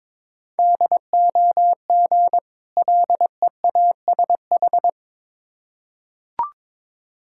Morse code, heard as a keyed tone, about 22 wpm, 690 Hz. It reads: DOG LEASH
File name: Difficult Words with Hints - 22wpm